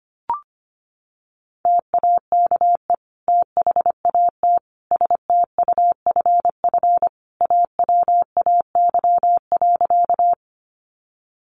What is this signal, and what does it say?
Morse code, heard as a keyed tone, about 25 wpm, 700 Hz. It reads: TAKE THAT STUFF AWAY.
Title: Sentences from Top 1400 Words 25wpm